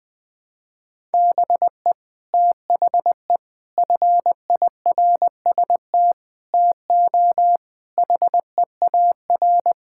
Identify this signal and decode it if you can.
Morse code, heard as a keyed tone, about 20 wpm, 700 Hz. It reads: BE THE FIRST TO HEAR
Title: Sets of 6 Words - 20wpm